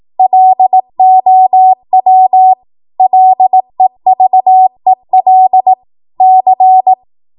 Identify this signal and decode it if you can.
Morse code, heard as a keyed tone, about 18 wpm, 750 Hz. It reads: LOW LEVEL C
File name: Morse / 18WPM